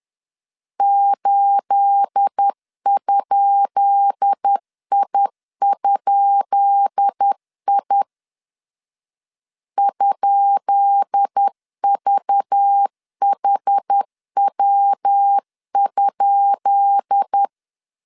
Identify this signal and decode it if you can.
Morse code, heard as a keyed tone, about 10 wpm, 800 Hz. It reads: 8?I?I ?VHW?
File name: LEZIONI / 10 SIGNS / 10B INT 10wpm 2844